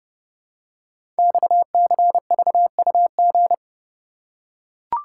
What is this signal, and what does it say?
Morse code, heard as a keyed tone, about 28 wpm, 700 Hz. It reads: XCVUG
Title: Code Groups - 5 Letter:Numbers - 30wpm